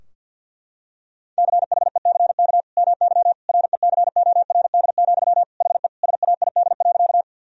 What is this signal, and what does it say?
Morse code, heard as a keyed tone, about 52 wpm, 700 Hz. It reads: CHECK MY RECORD, HE SAID.
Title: Sentences from Top 1000 Words - Review Part 3 - 50wpm